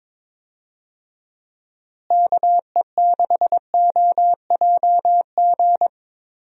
Morse code, heard as a keyed tone, about 22 wpm, 700 Hz. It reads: KE6OJG
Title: USA Call Signs - 6 Characters - Repeated 3x - 22wpm